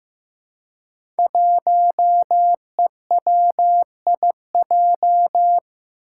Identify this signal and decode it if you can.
Morse code, heard as a keyed tone, about 15 wpm, 700 Hz. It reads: 1EWIJ
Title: Five Challenging Letter-Number Combinations 15wpm